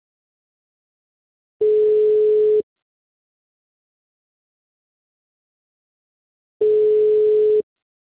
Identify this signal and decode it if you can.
Ringback tone (Europe, 425 Hz).
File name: out- -2006- - - .39618